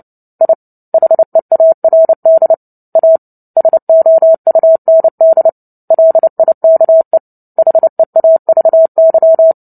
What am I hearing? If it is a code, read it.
Morse code, heard as a keyed tone, about 30 wpm, 650 Hz. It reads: I HEARD A SOUND LIKE HEAVY
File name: Book Two / 03-Chapter 3 - 29 WPM